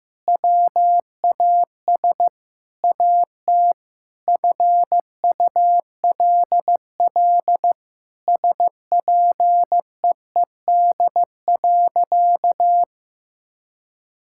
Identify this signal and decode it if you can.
Morse code, heard as a keyed tone, about 16 wpm, 700 Hz. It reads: WAS AT FULL SPEED.
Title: Sentences from Top 1000 Words - Review Part 5 - 15wpm